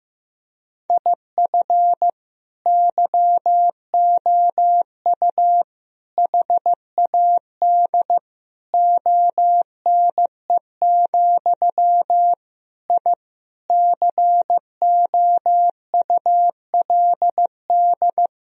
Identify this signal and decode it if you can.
Morse code, heard as a keyed tone, about 15 wpm, 700 Hz. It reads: IF YOU HAD ONE, I COULD
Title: Sentences from Top 1000 Words - Review Part 3 - 15wpm